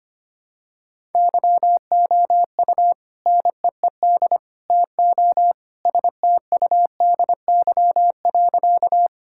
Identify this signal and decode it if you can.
Morse code, heard as a keyed tone, about 25 wpm, 700 Hz. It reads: YOU NEED TO STUDY.
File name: practice / 25wpm / Sentences with 3-Word Phrases - 25wpm